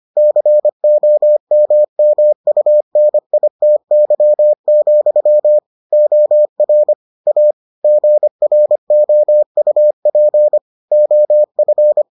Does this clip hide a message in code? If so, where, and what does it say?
Yes: Morse code from 0.2 to 12.0 s, heard as a keyed tone, about 25 wpm, 600 Hz. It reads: COMMUNITY, OR A GROUP OF